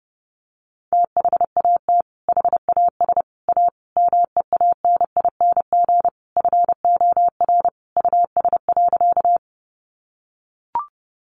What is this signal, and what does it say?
Morse code, heard as a keyed tone, about 30 wpm, 700 Hz. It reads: THAT HAS A MEANING FOR US.
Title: Sentences from Top 900 Words 30wpm